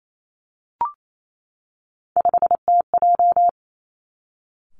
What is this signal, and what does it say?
Morse code, heard as a keyed tone, about 28 wpm, 700 Hz. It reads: 5TJ
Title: Three Challenging Letter-Number Combinations 28wpm